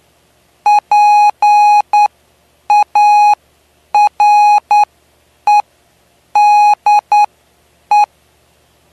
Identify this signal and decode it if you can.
Morse code, heard as a keyed tone, about 9 wpm, spaced as for 7 wpm, 830 Hz. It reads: PAREDE
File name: cw / py2lcd / 09-PY2LCD-Li��o 9